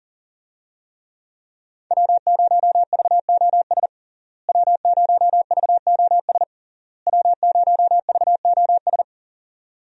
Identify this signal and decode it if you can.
Morse code, heard as a keyed tone, about 38 wpm, 700 Hz. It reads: W0VOS W0VOS W0VOS
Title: USA Call Signs - 5 Characters - Repeated 3x - 40wpm